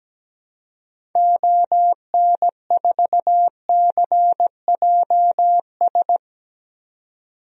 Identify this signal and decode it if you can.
Morse code, heard as a keyed tone, about 17 wpm, 700 Hz. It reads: ON4CJS
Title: DXCC-100 Call Signs - 6 Characters - 17wpm